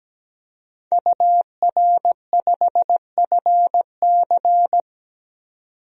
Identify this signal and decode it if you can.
Morse code, heard as a keyed tone, about 17 wpm, 700 Hz. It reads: UR5FC
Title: DXCC-100 Call Signs - 5 Characters - 17wpm